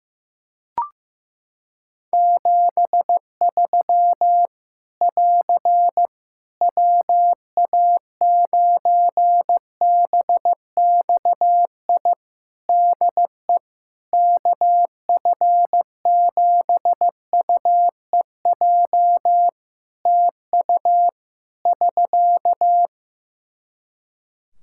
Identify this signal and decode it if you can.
Morse code, heard as a keyed tone, about 15 wpm, 700 Hz. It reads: 73 <AR> WA9BXI DE KF7UEJ TU <SK>
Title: K1USN SST QSO Practice 15wpm